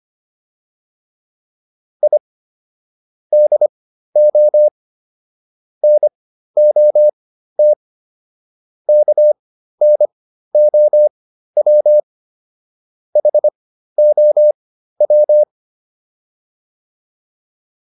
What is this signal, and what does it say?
Morse code, heard as a keyed tone, about 25 wpm, spaced as for 13 wpm, 600 Hz. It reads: I DO NOT KNOW HOW